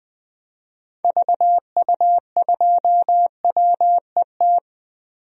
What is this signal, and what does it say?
Morse code, heard as a keyed tone, about 20 wpm, 700 Hz. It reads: VU2WET
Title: DXCC-100 Call Signs - 6 Characters - 20wpm